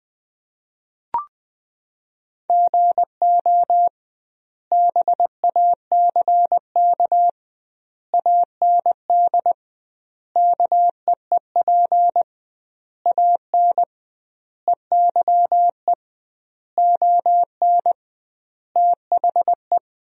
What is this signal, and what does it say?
Morse code, heard as a keyed tone, about 20 wpm, 700 Hz. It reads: GO BACK AND KEEP AN EYE ON THE
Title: 2x Word-Spacing - Sentences from Top 300 Words - 20wpm